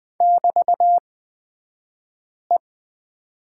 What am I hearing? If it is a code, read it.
Morse code, heard as a keyed tone, about 20 wpm, spaced as for 5 wpm, 700 Hz. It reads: <BT>E
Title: 242 Lesson 39d - Letter z - Two Characters 20wpm